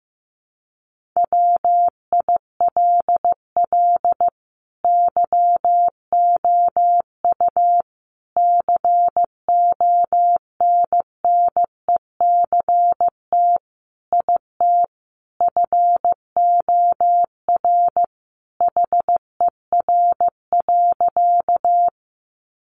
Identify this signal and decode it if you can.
Morse code, heard as a keyed tone, about 15 wpm, 700 Hz. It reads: WILL YOU CONNECT IT FOR HER.
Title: Sentences from Top 1500 Words 15wpm